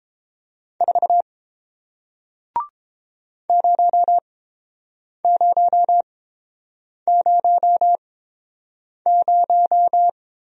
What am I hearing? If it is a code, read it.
Morse code, heard as a keyed tone, about 30 wpm, spaced as for 7 wpm, 700 Hz. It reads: 4 0000